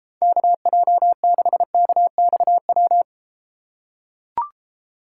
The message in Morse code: KJ6KXW

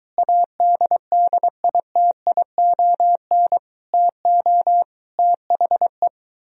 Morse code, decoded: ADDITION TO THE

